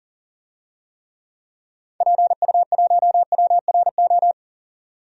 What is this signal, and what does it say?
Morse code, heard as a keyed tone, about 40 wpm, 700 Hz. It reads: PU1WRO
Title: DXCC-100 Call Signs - 6 Characters - 40wpm